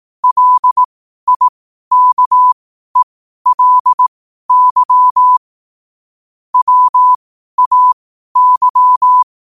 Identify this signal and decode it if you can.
Morse code, heard as a keyed tone, about 18 wpm, spaced as for 12 wpm, 1000 Hz. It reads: LIKELY WAY
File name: qotdfiles / 12wpm